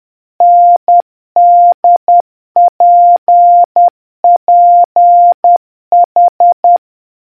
Morse code, heard as a keyed tone, about 10 wpm, 700 Hz. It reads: NDPPH